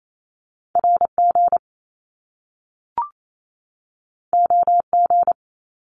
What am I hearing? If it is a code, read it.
Morse code, heard as a keyed tone, about 28 wpm, 700 Hz. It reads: RG OG
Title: Top Two Letter Combinations in Words 28wpm